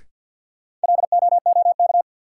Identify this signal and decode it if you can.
Morse code, heard as a keyed tone, about 54 wpm, 700 Hz. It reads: LOOK
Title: Sentences from Top 100 Words - New Word Drill - 50wpm